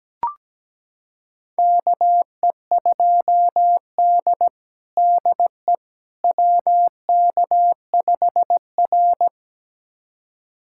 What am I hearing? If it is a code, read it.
Morse code, heard as a keyed tone, about 17 wpm, 700 Hz. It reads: KE2D DE WK5R
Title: USA Callsign DE Callsign - 4 Characters - 17wpm